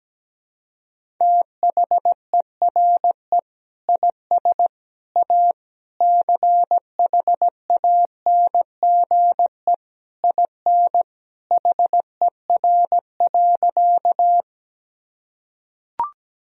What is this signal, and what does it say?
Morse code, heard as a keyed tone, about 17 wpm, 700 Hz. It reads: THERE IS A CHANGE IN HER.
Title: Sentences from Top 1000 Words - Review Part 1 - 17wpm